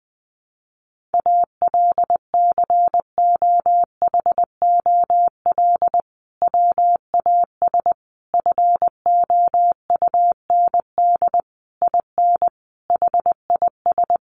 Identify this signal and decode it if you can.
Morse code, heard as a keyed tone, about 20 wpm, 700 Hz. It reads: ALCOHOL WAS FOUND IN HIS
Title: Sentences from Top 1400 Words 20wpm